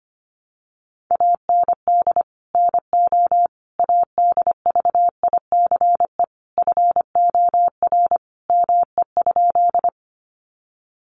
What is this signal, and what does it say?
Morse code, heard as a keyed tone, about 25 wpm, 700 Hz. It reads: AND NO ADVICE FOR ME?
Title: Sentences from Top 1100 Words 25wpm